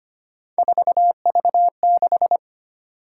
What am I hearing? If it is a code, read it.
Morse code, heard as a keyed tone, about 25 wpm, 700 Hz. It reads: 4V6
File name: Three Letter-Number Combinations - 25wpm